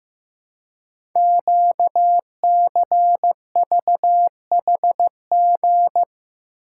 Morse code, heard as a keyed tone, about 15 wpm, 700 Hz. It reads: QCVHG
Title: Code Groups - 5 Letters - 15wpm